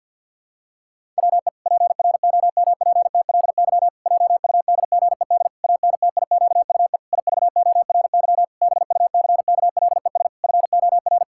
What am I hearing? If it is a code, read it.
Morse code, heard as a keyed tone, about 50 wpm, 700 Hz. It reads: WE PROMPTLY JUDGED ANTIQUE IVORY BUCKLES FOR
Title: Pangram Challenge v2 - 50wpm